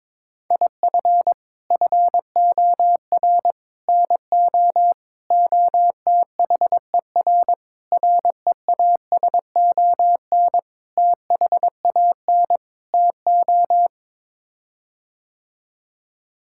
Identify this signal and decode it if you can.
Morse code, heard as a keyed tone, about 22 wpm, 700 Hz. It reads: IF FOR NO OTHER REASON THAN TO
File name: practice / 22wpm / Sets of 7 Words - 22wpm